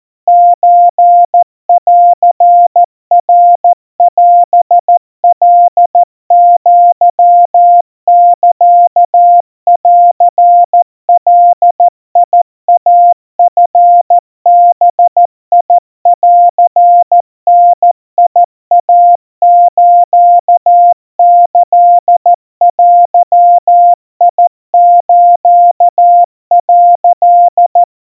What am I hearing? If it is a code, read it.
Morse code (Japanese Wabun code), heard as a keyed tone, about 14 wpm, 700 Hz. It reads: ソンナオカアサンガイチバンダイスキデス」
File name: kawa / Morse sample / kamyu 50 cpm 700 Hz Crispy